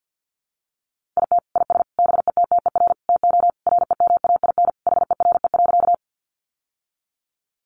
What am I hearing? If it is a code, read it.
Morse code, heard as a keyed tone, about 50 wpm, 700 Hz. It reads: IT IS BETTER TO REMAIN HERE.
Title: Sentences from Top 1200 Words 50wpm